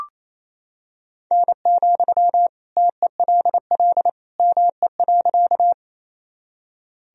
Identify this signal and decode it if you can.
Morse code, heard as a keyed tone, about 28 wpm, 700 Hz. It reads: N, TELL ME.
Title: Sentences from Top 200 Words and Abbreviations 28wpm